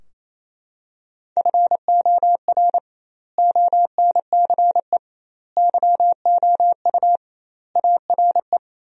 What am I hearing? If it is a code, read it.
Morse code, heard as a keyed tone, about 28 wpm, 700 Hz. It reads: FOR ONCE YOU ARE
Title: 2x Word-Spacing - Sentences from Top 300 Words - 28wpm